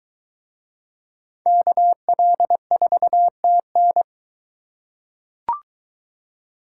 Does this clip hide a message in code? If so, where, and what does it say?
Yes: Morse code from 1.5 to 4.0 s, heard as a keyed tone, about 22 wpm, 700 Hz. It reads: KL4TN